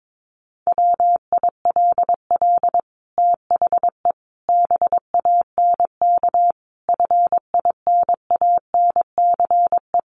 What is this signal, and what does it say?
Morse code, heard as a keyed tone, about 22 wpm, 700 Hz. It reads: WILL THE BANK FINANCE
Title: Sentences from Top 1500 Words 22wpm